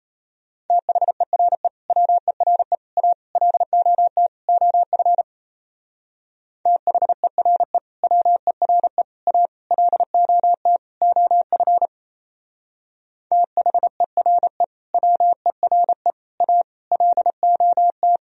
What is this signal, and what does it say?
Morse code, heard as a keyed tone, about 34 wpm, 700 Hz. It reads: THERE WERE A LOT OF THERE WERE A LOT OF THERE WERE A LOT
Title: Speed-Racing - Sets of 5 Words 28wpm